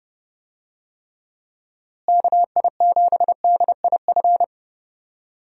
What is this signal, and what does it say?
Morse code, heard as a keyed tone, about 30 wpm, 700 Hz. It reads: KI7DIF
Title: USA Call Signs - 6 Characters - Repeated 3x - 30wpm